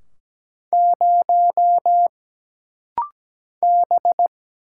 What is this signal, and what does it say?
Morse code, heard as a keyed tone, about 17 wpm, 700 Hz. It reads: 0 B
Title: Single Letter-Number - Rapid-Fire 17wpm